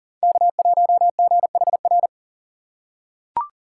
Morse code, heard as a keyed tone, about 40 wpm, 700 Hz. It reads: K1GHR
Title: USA Call Signs - 5 Characters - 6x Character Spacing - 40wpm